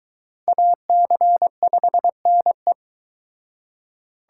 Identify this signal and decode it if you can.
Morse code, heard as a keyed tone, about 23 wpm, 700 Hz. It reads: AC5NE